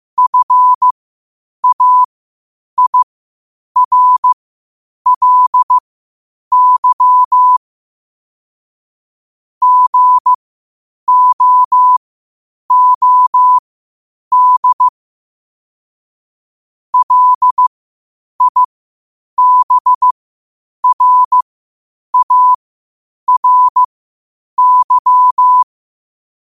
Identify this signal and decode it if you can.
Morse code, heard as a keyed tone, about 15 wpm, spaced as for 8 wpm, 1000 Hz. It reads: FAIRLY GOOD LIBRARY
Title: 8 WPM morse code quotes for Thu, 29 Feb 2024